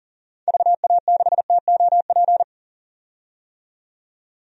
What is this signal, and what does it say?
Morse code, heard as a keyed tone, about 40 wpm, 700 Hz. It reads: VA6TOP
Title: DXCC-100 Call Signs - 6 Characters - 40wpm